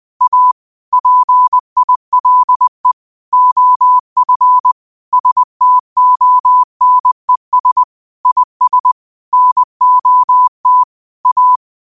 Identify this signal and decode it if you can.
Morse code, heard as a keyed tone, about 20 wpm, 1000 Hz. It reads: A PILE OF STONES IS NOT A
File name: qotdfiles / 20wpm